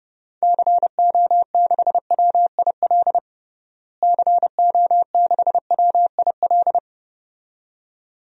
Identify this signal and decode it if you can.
Morse code, heard as a keyed tone, about 30 wpm, 700 Hz. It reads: CO6WIL CO6WIL
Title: DXCC-100 Call Signs - 6 Characters - Repeated 2x - 30wpm